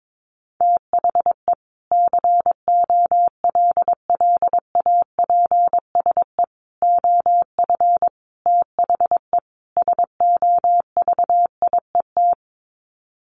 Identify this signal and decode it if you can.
Morse code, heard as a keyed tone, about 22 wpm, 700 Hz. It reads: THE COLLAPSE OF THE SOVIET
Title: Sets of 5 Words Encore 22wpm